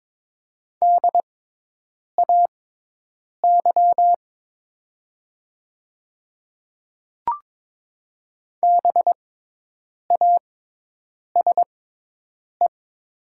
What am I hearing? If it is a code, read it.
Morse code, heard as a keyed tone, about 22 wpm, spaced as for 8 wpm, 700 Hz. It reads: DAY BASE
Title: 6x Character Spacing - Top 300 Words - 22wpm